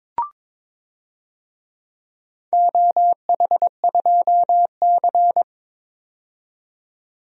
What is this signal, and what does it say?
Morse code, heard as a keyed tone, about 23 wpm, 700 Hz. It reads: OH2C
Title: DXCC-100 Call Signs - 4 Characters - 22wpm